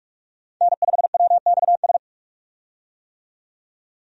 Morse code, heard as a keyed tone, about 46 wpm, 700 Hz. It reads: N5WXS
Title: USA Call Signs - 5 Characters - 45wpm